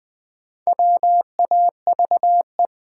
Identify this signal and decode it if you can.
Morse code, heard as a keyed tone, about 20 wpm, 700 Hz. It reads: WAVE
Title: Sentences from Top 1400 Words - New Word Drill Alt - 20wpm